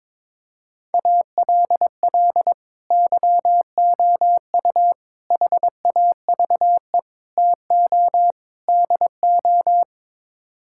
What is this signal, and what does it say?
Morse code, heard as a keyed tone, about 22 wpm, 700 Hz. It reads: ALL YOU HAVE TO DO
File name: practice / 22wpm / Build A Phrase - 5-Word Phrases - 22wpm